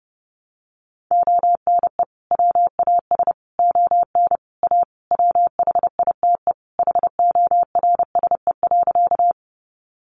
Morse code, heard as a keyed tone, about 30 wpm, 700 Hz. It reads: ONE WAS ON A WHITE HORSE.
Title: Sentences from Top 300 Words 30wpm